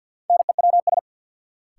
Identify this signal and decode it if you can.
Morse code, heard as a keyed tone, about 52 wpm, 700 Hz. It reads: NEWS
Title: Sets of 5 Words - New Word Drill - 50wpm